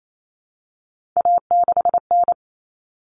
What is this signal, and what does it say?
Morse code, heard as a keyed tone, about 28 wpm, 700 Hz. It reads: A6N
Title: All Cut Numbers - 3-Digit 28wpm